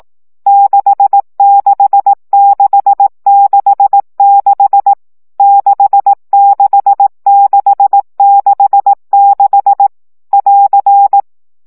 Morse code, heard as a keyed tone, about 18 wpm, 800 Hz. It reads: 66666 66666 <AR>
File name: CW gyakorlas 6 15 baud